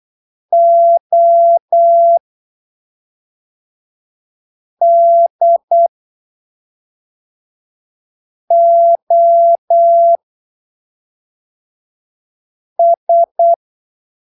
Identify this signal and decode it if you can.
Morse code, heard as a keyed tone, about 8 wpm, spaced as for 3 wpm, 680 Hz. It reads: ODOS
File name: arquivos / texto / texto 25 08ppm